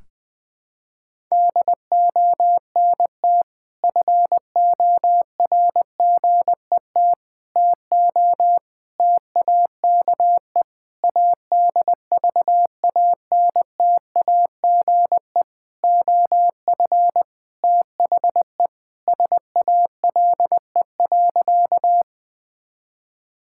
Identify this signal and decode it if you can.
Morse code, heard as a keyed tone, about 21 wpm, 700 Hz. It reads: DONT FORGET TO TAKE ADVANTAGE OF THE SALE.
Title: Sentences with 5-Word Phrases - 20wpm